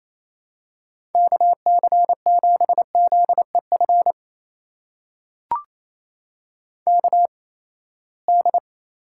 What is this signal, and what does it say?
Morse code, heard as a keyed tone, about 29 wpm, 700 Hz. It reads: K C 7 Z E F K D
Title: USA Call Signs - 6 Characters - 8x Character Spacing - 28wpm